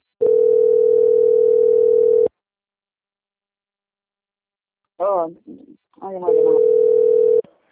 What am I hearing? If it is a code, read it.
Ringback tone (North America), 440 + 480 Hz, with speech.
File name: out- - - - - .12761